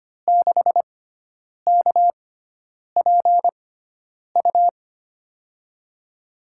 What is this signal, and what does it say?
Morse code, heard as a keyed tone, about 25 wpm, spaced as for 9 wpm, 700 Hz. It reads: 6KPU
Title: USA Call Signs - 6 Characters - 6x Character Spacing - 25wpm